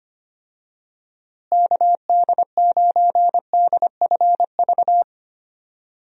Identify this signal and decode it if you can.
Morse code, heard as a keyed tone, about 25 wpm, 700 Hz. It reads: KD9DFV
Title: USA Call Signs - 6 Characters - Repeated 3x - 25wpm